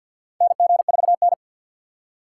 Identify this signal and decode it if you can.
Morse code, heard as a keyed tone, about 47 wpm, 700 Hz. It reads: NG4N
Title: USA Call Signs - 2x Character Spacing - 50wpm